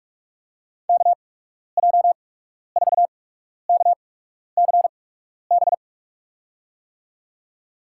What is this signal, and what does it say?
Morse code, heard as a keyed tone, about 47 wpm, spaced as for 12 wpm, 700 Hz. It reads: KJ4KCB
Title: USA Call Signs - 6 Characters - 8x Character Spacing - 45wpm